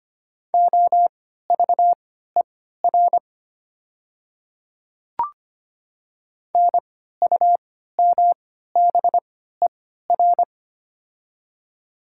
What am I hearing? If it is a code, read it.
Morse code, heard as a keyed tone, about 25 wpm, spaced as for 14 wpm, 700 Hz. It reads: OVER NUMBER
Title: 3x Character Spacing - Top 200 Words - 25wpm